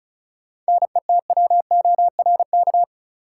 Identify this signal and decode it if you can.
Morse code, heard as a keyed tone, about 34 wpm, 700 Hz. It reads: NETWORK
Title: Sentences from Top 1000 Words - Review Part 7 - New Word Drill - 35wpm